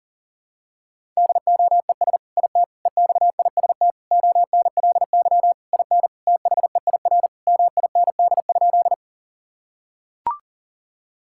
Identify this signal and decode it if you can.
Morse code, heard as a keyed tone, about 37 wpm, 700 Hz. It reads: DOES IT EXIST ONLY IN THEIR MIND?
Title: Sentences from Top 1000 Words - Review Part 5 - 40wpm